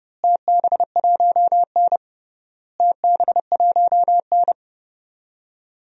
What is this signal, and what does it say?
Morse code, heard as a keyed tone, about 32 wpm, 700 Hz. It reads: TB1N TB1N
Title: DXCC-100 Call Signs - 4 Characters - Repeated 3x - 30wpm